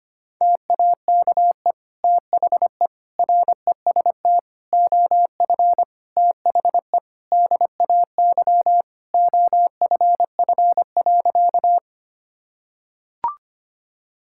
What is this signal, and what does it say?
Morse code, heard as a keyed tone, about 25 wpm, 700 Hz. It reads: TAKE THE REST OF THE DAY OFF.
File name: practice / 25wpm / Sentences with 2-Word Phrases - 25wpm